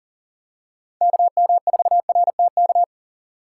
Morse code, heard as a keyed tone, about 40 wpm, 690 Hz. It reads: KM4RTK
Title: USA Call Signs - 6x Character Spacing - 40wpm